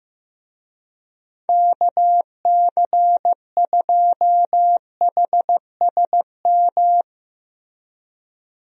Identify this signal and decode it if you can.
Morse code, heard as a keyed tone, about 15 wpm, 700 Hz. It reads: KC2HSM
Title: USA Call Signs - 6 Characters - Repeated 2x - 15wpm